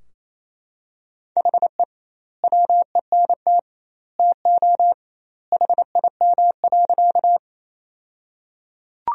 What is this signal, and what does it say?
Morse code, heard as a keyed tone, about 27 wpm, 700 Hz. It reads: HE WENT TO HIM.
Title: 2x Word-Spacing - Sentences from Top 200 Words - 28wpm